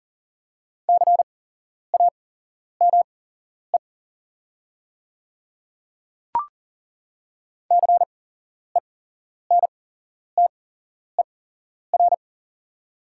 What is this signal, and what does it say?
Morse code, heard as a keyed tone, about 37 wpm, spaced as for 11 wpm, 690 Hz. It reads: CAME CENTER
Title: 8x Character Spacing - Top 500 Words - 40wpm